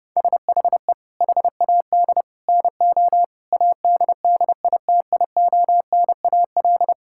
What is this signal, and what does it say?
Morse code, heard as a keyed tone, about 32 wpm, 700 Hz. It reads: SHE HAD NO ADDITIONAL
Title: Sentences from Top 1000 Words - Review Part 6 - 30wpm